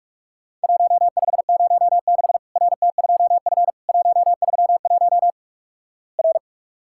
Morse code, heard as a keyed tone, about 48 wpm, 700 Hz. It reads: 1506 RT2F 131 R
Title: Worked All Europe DX Contest - QTC Exchange - 45wpm